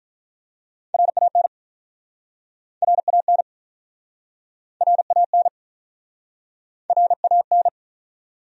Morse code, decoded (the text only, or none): RAN RAN RAN RAN